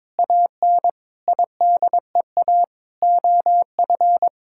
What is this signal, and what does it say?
Morse code, heard as a keyed tone, about 22 wpm, 700 Hz. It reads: AN IDEA OF
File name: practice / 22wpm / Sets of 7 Words - 22wpm